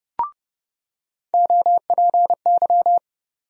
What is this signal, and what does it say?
Morse code, heard as a keyed tone, about 30 wpm, 700 Hz. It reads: OPY
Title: USA Call Sign Postfixes - 3 Letters 30wpm